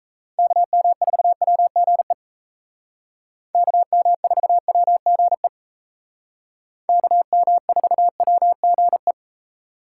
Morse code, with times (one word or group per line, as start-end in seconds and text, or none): 0.4-2.1 KM4WGE
3.5-5.5 KM4WGE
6.9-9.1 KM4WGE